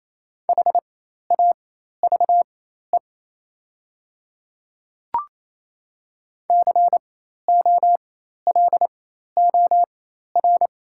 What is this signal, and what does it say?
Morse code, heard as a keyed tone, about 28 wpm, spaced as for 13 wpm, 700 Hz. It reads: HAVE COLOR
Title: 4x Character Spacing - Top 300 Words - 28wpm